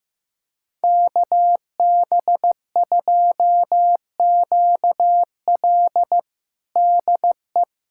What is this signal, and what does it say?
Morse code, heard as a keyed tone, about 15 wpm, 700 Hz. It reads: KB2QL DE
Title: USA Callsign DE Callsign - 5 Characters - 15wpm